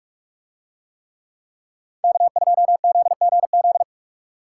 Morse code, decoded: K2ZGZ

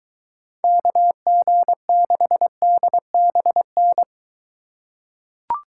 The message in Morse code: KG6DBN

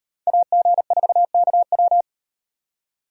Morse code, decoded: AG4KW

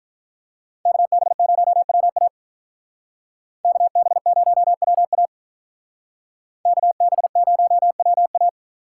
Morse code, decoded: KB0WA KB0WA KB0WA

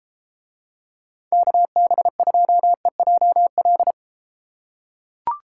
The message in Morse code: KB2EJL

